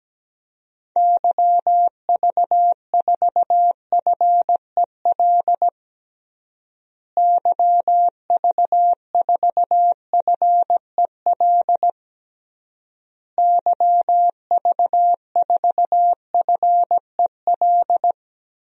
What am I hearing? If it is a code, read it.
Morse code, heard as a keyed tone, about 17 wpm, 700 Hz. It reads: YV4FEL YV4FEL YV4FEL
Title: DXCC-100 Call Signs - 6 Characters - Repeated 3x - 17wpm